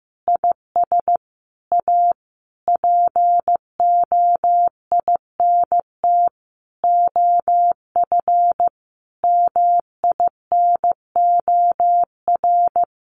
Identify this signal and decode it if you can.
Morse code, heard as a keyed tone, about 15 wpm, 700 Hz. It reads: IS A POINT OF MINOR
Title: Sentences from Top 1500 Words 15wpm